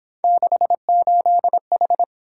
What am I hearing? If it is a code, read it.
Morse code, heard as a keyed tone, about 28 wpm, 700 Hz. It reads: 68H